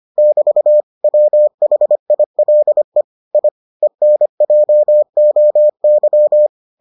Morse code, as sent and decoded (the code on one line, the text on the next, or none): -...- / .-- .... .. .-.. / .. / . -. .--- --- -.--
<BT> WHILE I ENJOY